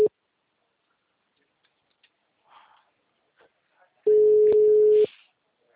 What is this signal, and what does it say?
Ringback tone (Europe, 425 Hz).
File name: out- -2009